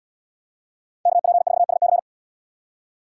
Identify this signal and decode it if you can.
Morse code, heard as a keyed tone, about 77 wpm, 700 Hz. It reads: KP4SC